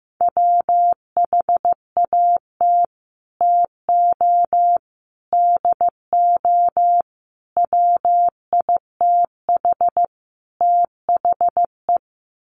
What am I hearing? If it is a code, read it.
Morse code, heard as a keyed tone, about 15 wpm, 700 Hz. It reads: WHAT TO DO WITH THE
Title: Sets of 5 Words Encore2 15wpm